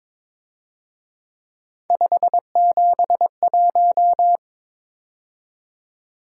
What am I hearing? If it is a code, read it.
Morse code, heard as a keyed tone, about 26 wpm, 700 Hz. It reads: 571
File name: Speed-Racing - Three Letter-Number Combinations - 22wpm